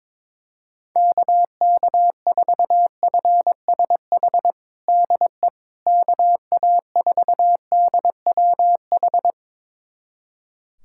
Morse code, heard as a keyed tone, about 22 wpm, 700 Hz. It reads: KK4FSH DE KA4DWH
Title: USA Callsign DE Callsign - 6 Characters - 22wpm